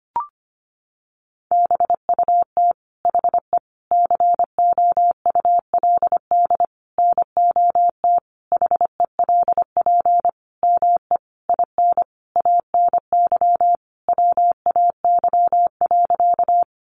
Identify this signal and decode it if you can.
Morse code, heard as a keyed tone, about 25 wpm, 700 Hz. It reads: BUT HE COULD NOT HELP ME IN ANY WAY.
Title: Sentences from Top 600 Words 25wpm